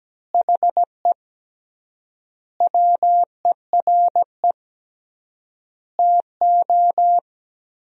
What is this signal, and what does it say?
Morse code, heard as a keyed tone, about 17 wpm, 700 Hz. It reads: HE WERE TO
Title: 3x Word-Spacing - Sentences from Top 400 Words - 17wpm